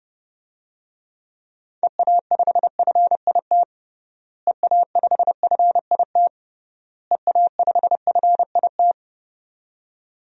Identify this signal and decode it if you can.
Morse code, heard as a keyed tone, about 30 wpm, 700 Hz. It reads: EA5FIT EA5FIT EA5FIT
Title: DXCC-100 Call Signs - 6 Characters - Repeated 3x - 30wpm